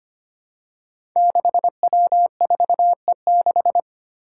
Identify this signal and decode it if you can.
Morse code, heard as a keyed tone, about 25 wpm, 700 Hz. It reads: 6W4E6
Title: Five Challenging Letter-Number Combinations 25wpm